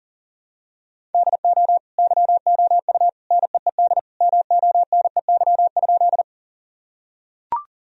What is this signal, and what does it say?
Morse code, heard as a keyed tone, about 40 wpm, 700 Hz. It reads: DO YOU NEED MONEY?
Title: Sentences from Top 1000 Words - Review Part 1 - 40wpm